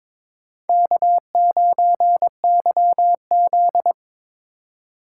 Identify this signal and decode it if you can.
Morse code, heard as a keyed tone, about 22 wpm, 700 Hz. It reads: K9YZ
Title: USA Call Signs - 4 Characters - 3x Character Spacing - 22wpm